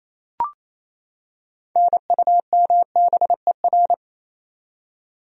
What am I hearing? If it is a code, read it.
Morse code, heard as a keyed tone, about 28 wpm, 700 Hz. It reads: NUMBER
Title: Top 100 Words 28wpm